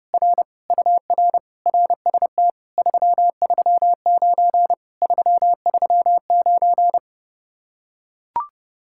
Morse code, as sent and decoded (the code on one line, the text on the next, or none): .-. / ..- .-. / .-. ... - / ...-- ...-- ----. / ...-- ...-- ----.
R UR RST 339 339